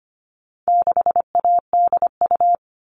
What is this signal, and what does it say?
Morse code, heard as a keyed tone, about 25 wpm, 700 Hz. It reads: 6ADU
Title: All Cut Numbers - 4-Digit 25wpm